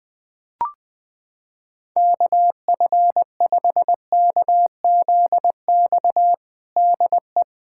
Morse code, heard as a keyed tone, about 21 wpm, 700 Hz. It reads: KF5KZX DE
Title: USA Callsign DE Callsign 20wpm